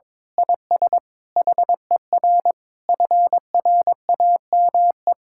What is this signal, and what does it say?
Morse code, heard as a keyed tone, about 22 wpm, 700 Hz. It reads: IS HER FRAME
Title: Sentences from Top 1000 Words - Review Part 6 - 22wpm